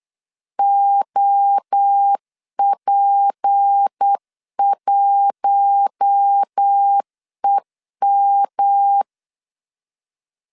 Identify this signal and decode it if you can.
Morse code, heard as a keyed tone, about 8 wpm, 800 Hz. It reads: OP1EM
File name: LEZIONI / 6 PKX0 / 6 P 8wpm 3025